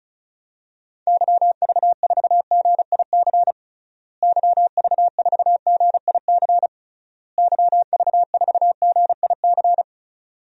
Morse code, heard as a keyed tone, about 36 wpm, 700 Hz. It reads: YV4GIC YV4GIC YV4GIC
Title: DXCC-100 Call Signs - 6 Characters - Repeated 3x - 35wpm